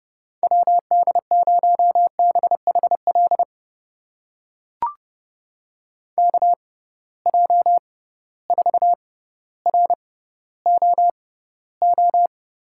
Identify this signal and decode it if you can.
Morse code, heard as a keyed tone, about 30 wpm, spaced as for 15 wpm, 700 Hz. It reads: WD0BHL KJ4ROO
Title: USA Call Signs - 6 Characters - 6x Character Spacing - 30wpm